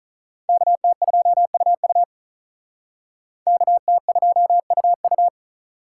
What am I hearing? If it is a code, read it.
Morse code, heard as a keyed tone, about 45 wpm, 700 Hz. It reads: KT2UU KT2UU